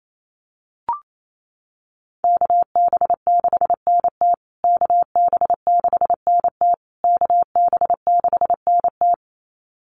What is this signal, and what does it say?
Morse code, heard as a keyed tone, about 28 wpm, 700 Hz. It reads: KB6NT KB6NT KB6NT
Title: USA Call Signs Repeated 3x 28wpm